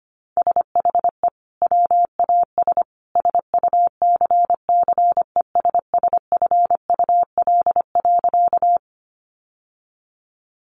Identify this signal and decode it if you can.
Morse code, heard as a keyed tone, about 25 wpm, 700 Hz. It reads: SHE WAS SUCCESSFUL.
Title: Sentences from Top 900 Words 25wpm